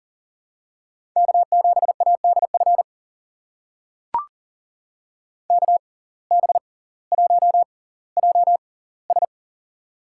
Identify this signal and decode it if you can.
Morse code, heard as a keyed tone, about 40 wpm, spaced as for 20 wpm, 700 Hz. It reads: K7ADF KB1JS
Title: USA Call Signs - 5 Characters - 6x Character Spacing - 40wpm